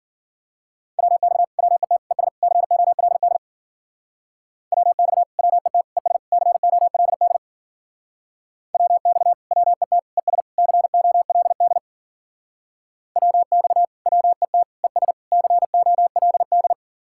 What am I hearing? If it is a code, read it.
Morse code, heard as a keyed tone, about 50 wpm, 700 Hz. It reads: WX WET ES COLD WX WET ES COLD WX WET ES COLD WX WET ES COLD